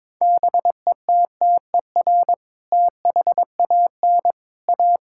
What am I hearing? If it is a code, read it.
Morse code, heard as a keyed tone, about 22 wpm, 700 Hz. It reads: BETTER THAN A
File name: practice / 22wpm / Sentences with 2-Word Phrases - 22wpm